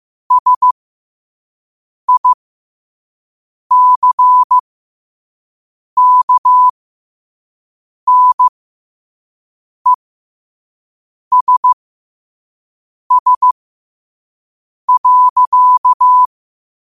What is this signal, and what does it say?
Morse code, heard as a keyed tone, about 15 wpm, spaced as for 5 wpm, 1000 Hz. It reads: SICKNESS.
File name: qotdfiles / 5wpm